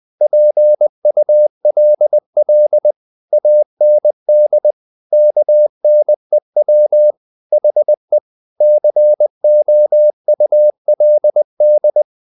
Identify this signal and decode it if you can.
Morse code, heard as a keyed tone, about 20 wpm, 600 Hz. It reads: PULL AND KNEW HE COULD
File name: the old man and the sea / the old man and the sea part 1 20wpm